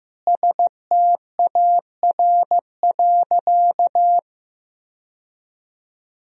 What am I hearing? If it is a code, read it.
Morse code, heard as a keyed tone, about 15 wpm, 700 Hz. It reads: STAR.
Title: Sentences from Top 1000 Words - Review Part 3 - 15wpm